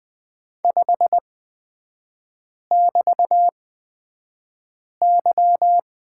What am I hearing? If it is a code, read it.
Morse code, heard as a keyed tone, about 20 wpm, spaced as for 5 wpm, 690 Hz. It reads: 5<BT>Y
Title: 229 Lesson 37e - Letter x - Three Characters 20wpm